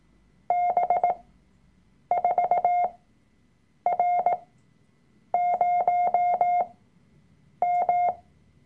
Morse code, heard as a keyed tone, about 18 wpm, spaced as for 7 wpm, 710 Hz. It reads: B4R0M
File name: Workshop Dateien / CW-Online-Kurs / Lektion 003